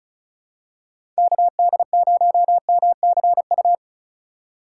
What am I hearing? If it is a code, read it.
Morse code, heard as a keyed tone, about 36 wpm, 700 Hz. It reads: KD0MCU
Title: USA Call Signs - 6 Characters - 4x Character Spacing - 35wpm